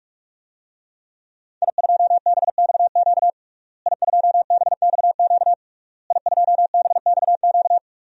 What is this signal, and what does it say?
Morse code, heard as a keyed tone, about 46 wpm, 700 Hz. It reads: I2BXQ I2BXQ I2BXQ
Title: DXCC-100 Call Signs - 5 Characters - Repeated 3x - 45wpm